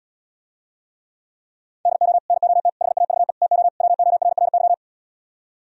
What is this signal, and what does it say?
Morse code, heard as a keyed tone, about 77 wpm, 700 Hz. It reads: DO NOT HAVE TO WORRY